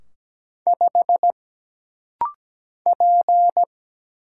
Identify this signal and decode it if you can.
Morse code, heard as a keyed tone, about 17 wpm, 700 Hz. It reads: 5 P
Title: Single Letter-Number - Rapid-Fire 17wpm